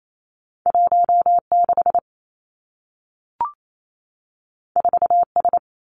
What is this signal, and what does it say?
Morse code, heard as a keyed tone, about 28 wpm, 700 Hz. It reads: 1 6 4 S
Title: Two Challenging Letter-Number Combinations 28wpm